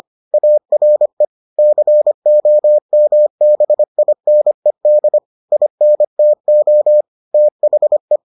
Morse code, heard as a keyed tone, about 25 wpm, 600 Hz. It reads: ARE COMBINED INTO THE